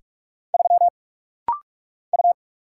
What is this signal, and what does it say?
Morse code, heard as a keyed tone, about 44 wpm, 700 Hz. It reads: 3 U
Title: Single Letter-Number Alt - Rapid-Fire 45wpm